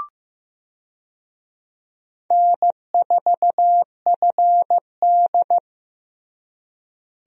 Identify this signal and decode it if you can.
Morse code, heard as a keyed tone, about 16 wpm, 700 Hz. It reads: N4FD
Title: USA Call Signs - 4 Characters - Repeated 2x - 15wpm